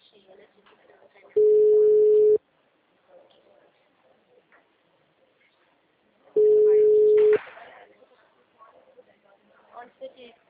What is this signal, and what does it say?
Ringback tone (Europe, 425 Hz), with speech.